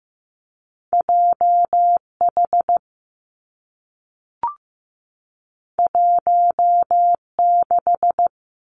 Morse code, heard as a keyed tone, about 15 wpm, 700 Hz. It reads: J H 1 6
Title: Two Challenging Letter-Number Combinations 15wpm